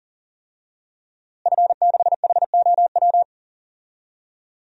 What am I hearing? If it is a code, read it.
Morse code, heard as a keyed tone, about 40 wpm, 700 Hz. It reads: F6HOW
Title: DXCC-100 Call Signs - 5 Characters - Repeated 2x - 40wpm